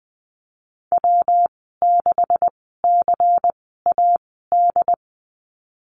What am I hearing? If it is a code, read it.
Morse code, heard as a keyed tone, about 20 wpm, spaced as for 14 wpm, 700 Hz. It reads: W6CAD
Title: 140 Lesson 23g - Letter f - Call Signs 20wpm